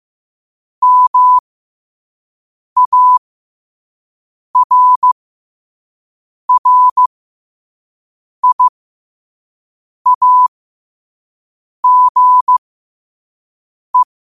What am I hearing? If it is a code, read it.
Morse code, heard as a keyed tone, about 15 wpm, spaced as for 5 wpm, 1000 Hz. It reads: MARRIAGE
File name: qotdfiles / 5wpm